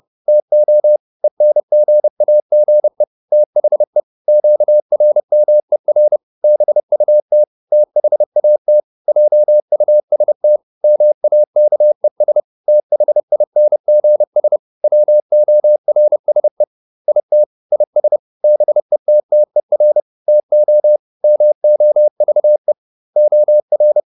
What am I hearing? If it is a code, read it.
Morse code, heard as a keyed tone, about 30 wpm, 600 Hz. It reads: TO ENGAGE THE QRMER BUT THAT JUST MAKES THINGS WORSE IT IS BETTER TO MOVE OR